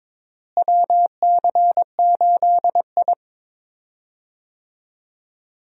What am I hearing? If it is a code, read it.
Morse code, heard as a keyed tone, about 22 wpm, 700 Hz. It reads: WC8I
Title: USA Call Signs - 4 Characters - Repeated 2x - 22wpm